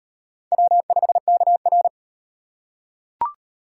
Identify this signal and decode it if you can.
Morse code, heard as a keyed tone, about 35 wpm, 700 Hz. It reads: W5KR